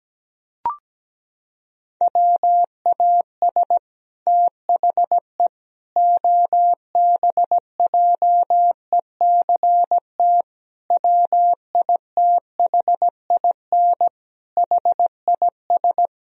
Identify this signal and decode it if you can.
Morse code, heard as a keyed tone, about 17 wpm, 700 Hz. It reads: WAS THE OBJECT WITHIN HIS
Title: Sentences from Top 1000 Words - Review Part 4 - 17wpm